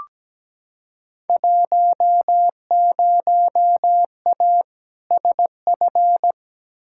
Morse code, heard as a keyed tone, about 17 wpm, 700 Hz. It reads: 10A SF
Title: Field Day Exchanges Encore - 17wpm